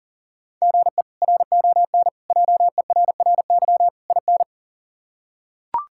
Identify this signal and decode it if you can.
Morse code, heard as a keyed tone, about 40 wpm, 700 Hz. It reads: GE RON JERRY IN
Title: K1USN SST QSO Practice - 40wpm